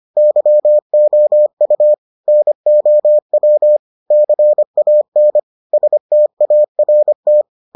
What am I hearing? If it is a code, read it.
Morse code, heard as a keyed tone, about 25 wpm, 600 Hz. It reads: YOU NOW CAN START